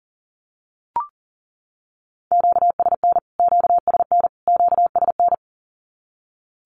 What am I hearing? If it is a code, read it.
Morse code, heard as a keyed tone, about 40 wpm, 700 Hz. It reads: QSN QSN QSN
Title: Q-Codes Repeated 3x 40wpm